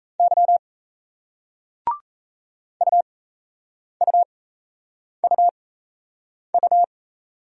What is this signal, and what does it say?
Morse code, heard as a keyed tone, about 37 wpm, spaced as for 8 wpm, 700 Hz. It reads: Y UUUU